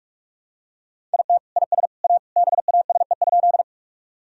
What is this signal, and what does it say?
Morse code, heard as a keyed tone, about 41 wpm, 700 Hz. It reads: IT IS A BASE?
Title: Sentences from Top 1000 Words - Review Part 1 - 45wpm